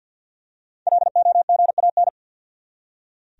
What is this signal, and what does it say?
Morse code, heard as a keyed tone, about 46 wpm, 700 Hz. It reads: LOGAN
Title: USA Common Names - 50wpm